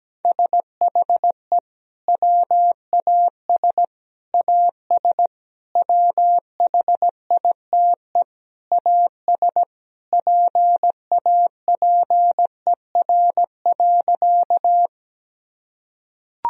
Morse code, heard as a keyed tone, about 17 wpm, 700 Hz. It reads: SHE WAS AS WHITE AS PAPER.
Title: Sentences from Top 1000 Words - Review Part 1 - 17wpm